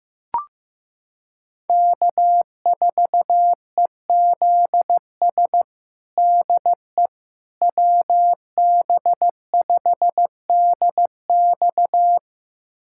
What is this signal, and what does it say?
Morse code, heard as a keyed tone, about 15 wpm, 700 Hz. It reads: K4EZS DE WB5DX
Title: USA Callsign DE Callsign - 5 Characters - 15wpm